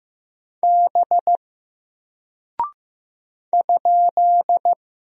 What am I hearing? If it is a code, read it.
Morse code, heard as a keyed tone, about 15 wpm, 700 Hz. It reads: B ?
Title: Single Character (Letter:Number:Punctuation:Prosign) Alt 15wpm